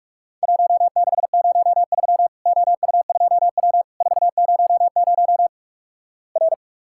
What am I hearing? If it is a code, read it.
Morse code, heard as a keyed tone, about 46 wpm, 700 Hz. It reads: 1603 OU2W 400 R
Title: Worked All Europe DX Contest - QTC Exchange - 45wpm